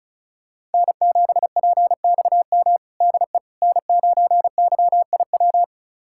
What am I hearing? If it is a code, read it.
Morse code, heard as a keyed tone, about 36 wpm, 700 Hz. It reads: N7PXM DE N9YIW
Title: USA Callsign DE Callsign - 5 Characters - 35wpm